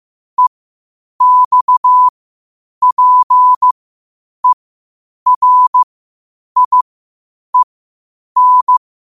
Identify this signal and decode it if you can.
Morse code, heard as a keyed tone, about 15 wpm, spaced as for 8 wpm, 1000 Hz. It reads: EXPERIEN
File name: qotdfiles / 8wpm